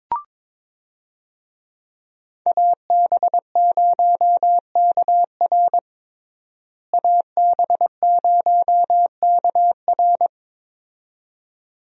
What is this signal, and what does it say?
Morse code, heard as a keyed tone, about 21 wpm, 700 Hz. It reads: AB0KR AB0KR
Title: USA Call Signs - 5 Characters - Repeated 2x - 22wpm